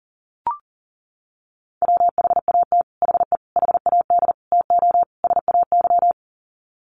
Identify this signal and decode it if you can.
Morse code, heard as a keyed tone, about 40 wpm, 700 Hz. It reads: WHAT HE HAD TO SAY
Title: Sets of 5 Words Encore2 40wpm